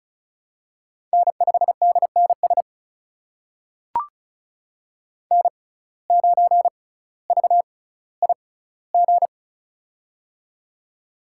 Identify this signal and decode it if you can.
Morse code, heard as a keyed tone, about 36 wpm, spaced as for 18 wpm, 700 Hz. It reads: N5DNS N9VIG
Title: USA Call Signs - 5 Characters - 6x Character Spacing - 35wpm